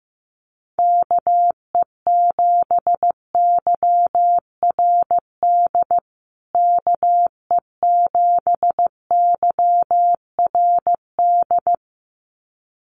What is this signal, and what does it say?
Morse code, heard as a keyed tone, about 15 wpm, 700 Hz. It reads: KE7YRD KE7YRD
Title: USA Call Signs Repeated 2x 15wpm